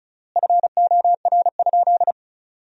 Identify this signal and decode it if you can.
Morse code, heard as a keyed tone, about 34 wpm, 700 Hz. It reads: FOR?
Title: 3x Word-Spacing - Sentences from Top 200 Words - 35wpm